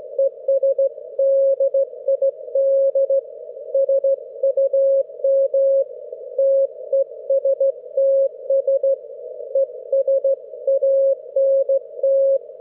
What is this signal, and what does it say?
Morse code, heard as a keyed tone, about 16 wpm, 540 Hz. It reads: ES DID SUM TESTS ES ANT
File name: GW3UEP / EI6DN 10am31May12